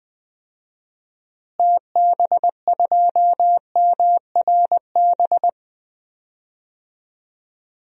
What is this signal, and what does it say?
Morse code, heard as a keyed tone, about 20 wpm, 700 Hz. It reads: TB2MRB
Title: DXCC-100 Call Signs - 6 Characters - Repeated 2x - 20wpm